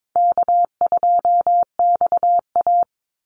Morse code, heard as a keyed tone, about 22 wpm, 690 Hz. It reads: K2XA
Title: Easy Call Signs Repeated 2x - 22wpm